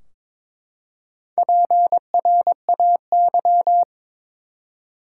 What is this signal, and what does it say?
Morse code, heard as a keyed tone, about 22 wpm, 700 Hz. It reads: PRAY
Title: Four Letter Words - 22wpm